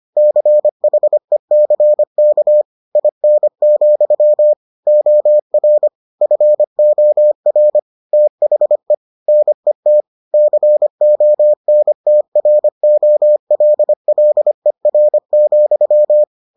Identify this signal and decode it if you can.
Morse code, heard as a keyed tone, about 25 wpm, 600 Hz. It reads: CHECK IN, OR FOR THE NET CONTROLLER,